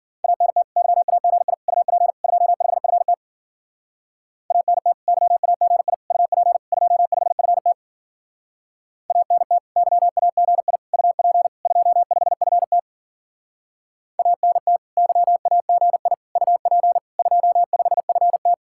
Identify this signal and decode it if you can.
Morse code, heard as a keyed tone, about 50 wpm, 700 Hz. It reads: ANT YAGI UP 25FT ANT YAGI UP 25FT ANT YAGI UP 25FT ANT YAGI UP 25FT
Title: Speed-Racing Common QSO Phrases - 40wpm